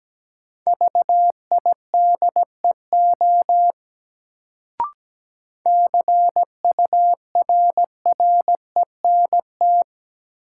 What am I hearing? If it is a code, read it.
Morse code, heard as a keyed tone, about 17 wpm, 700 Hz. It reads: VIDEO CURRENT
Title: Sentences from Top 1000 Words - Review Part 4 - New Word Drill - 17wpm